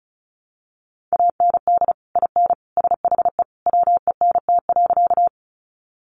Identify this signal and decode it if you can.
Morse code, heard as a keyed tone, about 34 wpm, 700 Hz. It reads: AND IN SHE WENT.
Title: Sentences from Top 200 Words 35wpm